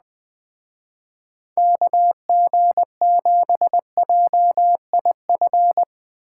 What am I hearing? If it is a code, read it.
Morse code, heard as a keyed tone, about 19 wpm, 700 Hz. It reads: KG7JIF
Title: Speed Racing - USA Call Signs 17wpm